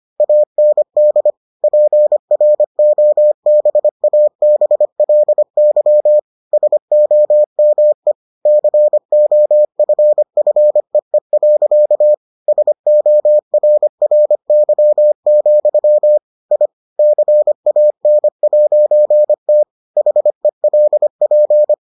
Morse code, heard as a keyed tone, about 25 wpm, 600 Hz. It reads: AND PROBABLY SOME COFFEE. SORRY, I CAN'T HELP